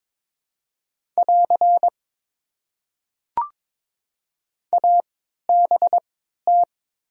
Morse code, heard as a keyed tone, about 22 wpm, spaced as for 12 wpm, 700 Hz. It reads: <AR> ABT
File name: Top 90 QSO Elements - 3x Character Spacing - 22wpm